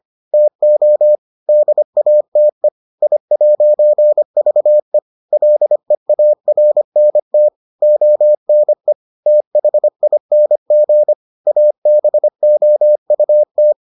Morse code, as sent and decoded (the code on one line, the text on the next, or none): - --- / -.. .- - . / .. .----. ...- . / .-.. .- .-. -. - / --- -. / - .... .. -. --. / .- -... --- ..- -
TO DATE I'VE LEARNT ONE THING ABOUT